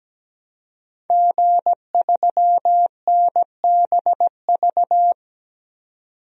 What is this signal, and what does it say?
Morse code, heard as a keyed tone, about 17 wpm, 700 Hz. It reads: G3NBV
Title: DXCC-100 Call Signs - 5 Characters - 17wpm